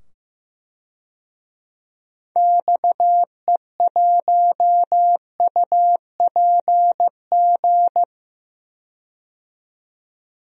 Morse code, heard as a keyed tone, about 15 wpm, 700 Hz. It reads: XE1UPG
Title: DXCC-100 Call Signs - 6 Characters - 15wpm